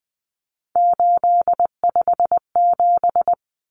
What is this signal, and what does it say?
Morse code, heard as a keyed tone, about 20 wpm, 700 Hz. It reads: 857
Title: Common Cut Numbers - 3-Digit 20wpm